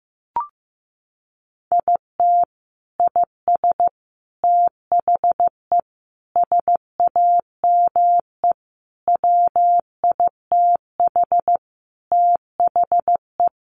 Morse code, heard as a keyed tone, about 15 wpm, 700 Hz. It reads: IT IS THE SAME WITH THE
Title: Sentences from Top 1100 Words 15wpm